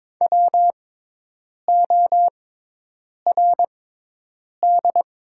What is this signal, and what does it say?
Morse code, heard as a keyed tone, about 22 wpm, spaced as for 8 wpm, 700 Hz. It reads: WORD
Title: 6x Character Spacing - Top 100 Words - 22wpm